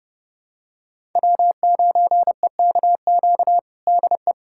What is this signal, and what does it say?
Morse code, heard as a keyed tone, about 30 wpm, 700 Hz. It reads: W9EKQ DE
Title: USA Callsign DE Callsign - 5 Characters - 30wpm